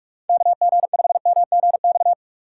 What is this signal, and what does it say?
Morse code, heard as a keyed tone, about 44 wpm, 700 Hz. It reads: KG5MGX
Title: USA Call Signs - 6 Characters - 4x Character Spacing - 45wpm